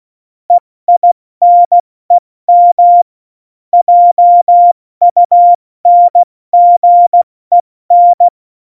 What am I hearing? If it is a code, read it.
Morse code, heard as a keyed tone, about 16 wpm, 700 Hz. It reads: EINEM JUNGEN
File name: Morsekurs / Witze / Witz 11 August 2020, 70 F80